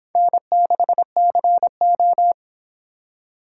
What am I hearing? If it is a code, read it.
Morse code, heard as a keyed tone, about 30 wpm, 700 Hz. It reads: N6CO